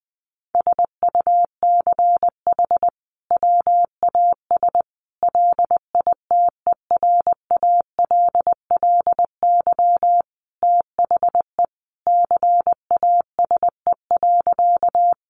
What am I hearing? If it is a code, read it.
Morse code, heard as a keyed tone, about 20 wpm, 700 Hz. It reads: SUCH WAS LITERALLY THE CASE.
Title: Sentences from Top 1400 Words 20wpm